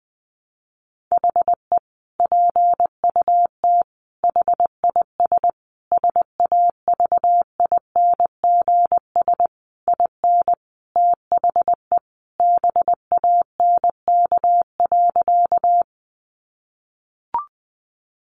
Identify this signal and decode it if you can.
Morse code, heard as a keyed tone, about 20 wpm, 700 Hz. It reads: HE PUT HIS SAVINGS IN THE BANK.
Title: Sentences from Top 1200 Words 20wpm